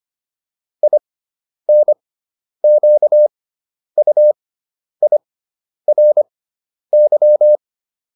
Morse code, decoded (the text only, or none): INQUIRY